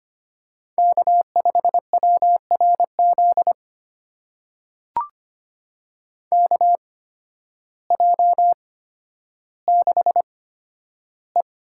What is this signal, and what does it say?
Morse code, heard as a keyed tone, about 25 wpm, spaced as for 12 wpm, 700 Hz. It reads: K5WRZ KJ6E
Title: USA Call Signs - 8x Character Spacing - 25wpm